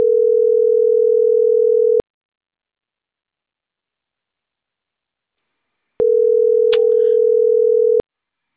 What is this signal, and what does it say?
Ringback tone (North America), 440 + 480 Hz.